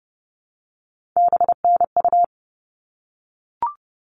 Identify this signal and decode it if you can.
Morse code, heard as a keyed tone, about 30 wpm, 700 Hz. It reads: BNU
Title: All Cut Numbers - 3-Digit 30wpm